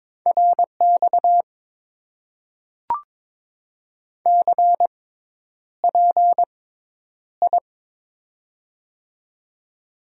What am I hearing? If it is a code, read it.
Morse code, heard as a keyed tone, about 22 wpm, spaced as for 10 wpm, 700 Hz. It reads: RX CPI
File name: Top 90 QSO Elements - 6x Character Spacing - 22wpm